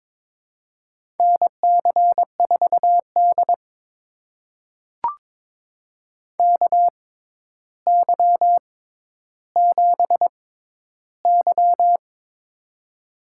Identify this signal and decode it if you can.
Morse code, heard as a keyed tone, about 22 wpm, spaced as for 11 wpm, 700 Hz. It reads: NC4D KY7Y
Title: USA Call Signs - 4 Characters - 6x Character Spacing - 22wpm